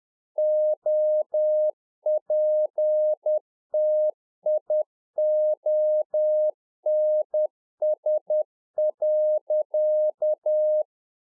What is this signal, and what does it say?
Morse code, heard as a keyed tone, about 10 wpm, 610 Hz. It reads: OPTIONS.
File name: 2011-04-mp3 / Czech Army Morse Keys 10WPM